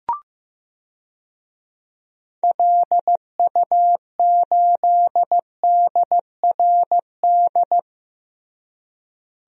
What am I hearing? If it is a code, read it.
Morse code, heard as a keyed tone, about 15 wpm, 700 Hz. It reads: LU8DRD
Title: DXCC-100 Call Signs - 6 Characters - Repeated 3x - 15wpm